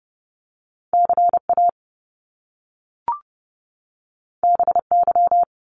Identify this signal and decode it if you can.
Morse code, heard as a keyed tone, about 30 wpm, 700 Hz. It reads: CA BY
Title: Top Two Letter Combinations in Words 30wpm